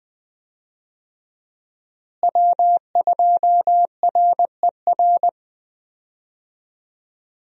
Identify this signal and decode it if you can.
Morse code, heard as a keyed tone, about 20 wpm, 700 Hz. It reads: W2RER
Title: USA Call Signs - 5 Characters - Repeated 2x - 20wpm